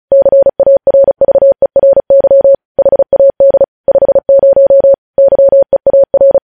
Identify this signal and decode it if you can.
Morse code, heard as a keyed tone, about 36 wpm, 550 Hz. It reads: CARVERY HAD 50 YEAR
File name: podcast / News headlines-1x-35x35